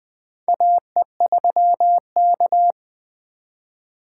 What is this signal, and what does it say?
Morse code, heard as a keyed tone, about 20 wpm, 700 Hz. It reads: AE3K
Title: USA Call Signs - 4 Characters - 8x Character Spacing - 20wpm